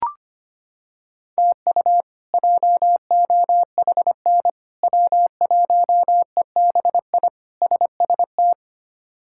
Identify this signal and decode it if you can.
Morse code, heard as a keyed tone, about 25 wpm, 700 Hz. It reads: TU JOHN W1EBI SST
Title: K1USN SST QSO Practice - 25wpm